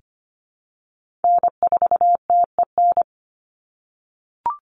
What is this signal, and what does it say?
Morse code, heard as a keyed tone, about 25 wpm, 700 Hz. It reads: N4TEN
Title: USA Call Signs 25wpm